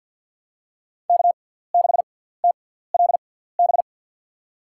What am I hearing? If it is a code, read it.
Morse code, heard as a keyed tone, about 53 wpm, spaced as for 17 wpm, 700 Hz. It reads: K6TLB
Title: USA Call Signs - 5 Characters - 6x Character Spacing - 50wpm